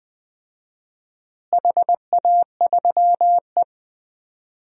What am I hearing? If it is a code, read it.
Morse code, heard as a keyed tone, about 20 wpm, 700 Hz. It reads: HA3E
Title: DXCC-100 Call Signs - 4 Characters - Repeated 2x - 20wpm